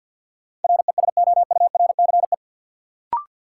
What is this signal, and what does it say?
Morse code, heard as a keyed tone, about 53 wpm, 700 Hz. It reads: RESOURCE
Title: Sentences from Top 1300 Words - New Word Drill Alt - 50wpm